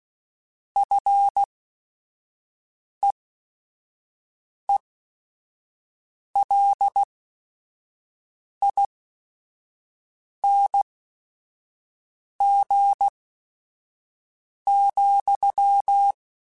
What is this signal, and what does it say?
Morse code, heard as a keyed tone, about 16 wpm, spaced as for 5 wpm, 800 Hz. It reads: FEELING,
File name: Short Runs / Short Runs-mp3s / SinglePrefixes @5